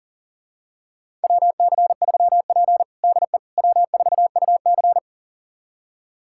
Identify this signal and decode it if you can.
Morse code, heard as a keyed tone, about 40 wpm, 700 Hz. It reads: WC3P DE W4UC
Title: USA Callsign DE Callsign - 4 Characters - 40wpm